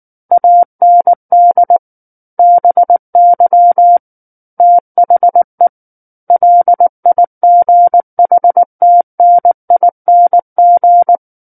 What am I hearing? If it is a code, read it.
Morse code, heard as a keyed tone, about 19 wpm, 700 Hz. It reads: AND BY THE LIGHTNING
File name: Book One / 10-Chapter 10 - 19 WPM